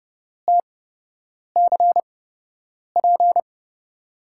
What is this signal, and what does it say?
Morse code, heard as a keyed tone, about 30 wpm, spaced as for 8 wpm, 700 Hz. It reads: TCP
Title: USA Call Signs - 5 Characters - 8x Character Spacing - 30wpm